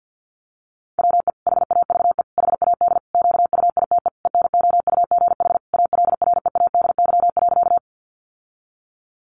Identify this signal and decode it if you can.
Morse code, heard as a keyed tone, about 50 wpm, 700 Hz. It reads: WE HAVE HAD QUITE ENOUGH ALREADY.
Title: Sentences from Top 700 Words 50wpm